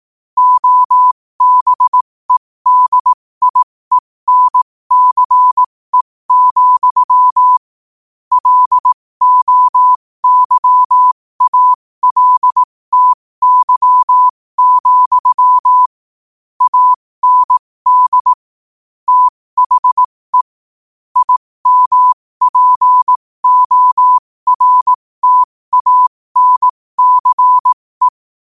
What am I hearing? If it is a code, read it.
Morse code, heard as a keyed tone, about 18 wpm, spaced as for 15 wpm, 1000 Hz. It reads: OBEDIENCE, LOYALTY, AND THE IMPORTANCE
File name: qotdfiles / 15wpm